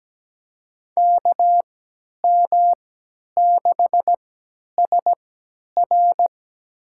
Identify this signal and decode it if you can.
Morse code, heard as a keyed tone, about 17 wpm, spaced as for 10 wpm, 700 Hz. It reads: KM6SR
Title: USA Call Signs - 5 Characters - 3x Character Spacing - 17wpm